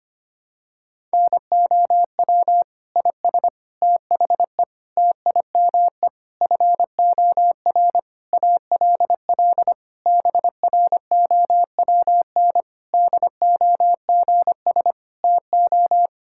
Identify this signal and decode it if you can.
Morse code, heard as a keyed tone, about 25 wpm, 700 Hz. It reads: NOW IS THE TIME FOR ALL BROWN DOGS TO
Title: Pangram Challenge v2 - 25wpm